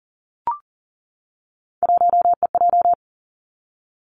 Morse code, heard as a keyed tone, about 38 wpm, 700 Hz. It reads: 1EJ
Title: Three Challenging Letter-Number Combinations 40wpm